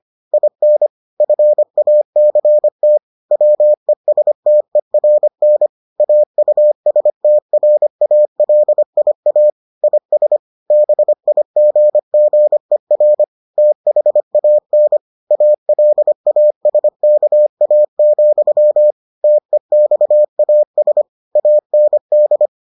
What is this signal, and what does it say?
Morse code, heard as a keyed tone, about 25 wpm, 600 Hz. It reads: IN FACT WESTERN AUSTRALIA IS BIGGER THAN ALASKA, TEXAS AND